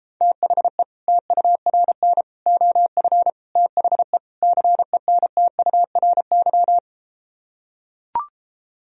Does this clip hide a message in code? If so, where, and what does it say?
Yes: Morse code from 0.2 to 6.8 s, heard as a keyed tone, about 33 wpm, 700 Hz. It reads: THE TURN OF THE CENTURY